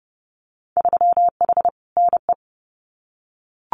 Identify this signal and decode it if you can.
Morse code, heard as a keyed tone, about 30 wpm, 700 Hz. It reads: 3H NE
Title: Winter Field Day Exchanges Encore 30wpm